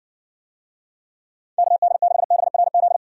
Morse code, heard as a keyed tone, about 60 wpm, 700 Hz. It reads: KD6BRZ